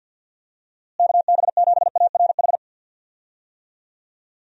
Morse code, heard as a keyed tone, about 52 wpm, 700 Hz. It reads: KB7ARH
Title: USA Call Signs - 6 Characters - Repeated 3x - 50wpm